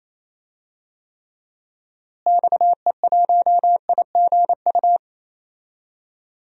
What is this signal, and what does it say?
Morse code, heard as a keyed tone, about 28 wpm, 700 Hz. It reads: XE1IGU
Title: DXCC-100 Call Signs - 6 Characters - Repeated 3x - 28wpm